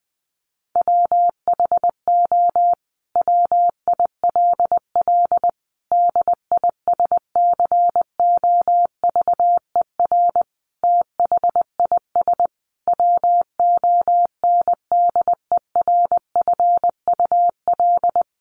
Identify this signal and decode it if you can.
Morse code, heard as a keyed tone, about 20 wpm, 700 Hz. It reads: WHO WILL DISCOVER THIS WONDERFUL
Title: Sentences from Top 1100 Words 20wpm